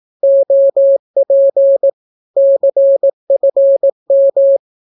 Morse code, heard as a keyed tone, about 18 wpm, 550 Hz. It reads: OP CFM
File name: naukapodstaw / skroty / skroty